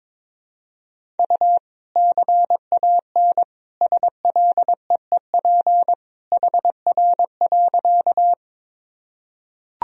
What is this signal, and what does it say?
Morse code, heard as a keyed tone, about 22 wpm, 700 Hz. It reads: U CAN SLEEP HR.
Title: Sentences from Top 500 Words and Abbreviations 22wpm